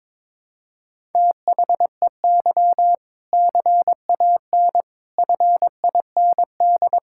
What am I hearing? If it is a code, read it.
Morse code, heard as a keyed tone, about 22 wpm, 700 Hz. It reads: THEY CAN FIND
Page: morse-code-ninja-practice-22wpm